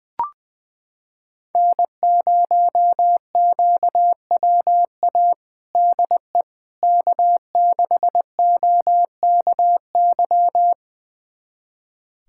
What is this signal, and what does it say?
Morse code, heard as a keyed tone, about 20 wpm, 700 Hz. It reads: N0QWA DE K6OKY
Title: USA Callsign DE Callsign - 5 Characters - 20wpm